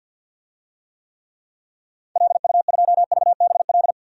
Morse code, heard as a keyed tone, about 52 wpm, 700 Hz. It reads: LU2VBL
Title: DXCC-100 Call Signs - 6 Characters - 50wpm